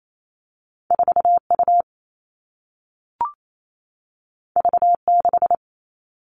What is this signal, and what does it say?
Morse code, heard as a keyed tone, about 28 wpm, 700 Hz. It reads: 4 U V 6
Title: Two Challenging Letter-Number Combinations 28wpm